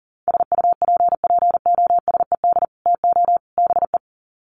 Morse code, heard as a keyed tone, about 40 wpm, 700 Hz. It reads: SUPPOSED TO BE
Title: Sets of 3 Words 40wpm